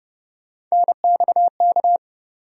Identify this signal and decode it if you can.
Morse code, heard as a keyed tone, about 30 wpm, 700 Hz. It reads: NXK
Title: USA Call Sign Postfixes - 3 Letters 30wpm